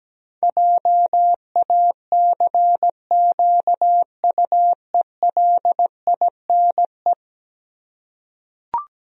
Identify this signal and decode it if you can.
Morse code, heard as a keyed tone, about 17 wpm, 700 Hz. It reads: JACQUELINE
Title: USA Common Names - 17wpm